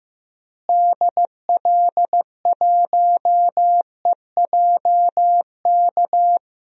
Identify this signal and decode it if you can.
Morse code, heard as a keyed tone, about 15 wpm, 700 Hz. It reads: DL1EJK
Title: DXCC-100 Call Signs - 6 Characters - Repeated 3x - 15wpm